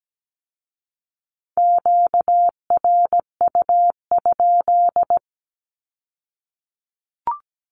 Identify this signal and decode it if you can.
Morse code, heard as a keyed tone, about 17 wpm, 700 Hz. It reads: QRU?
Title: Q-Codes with Spoken Meaning - 17wpm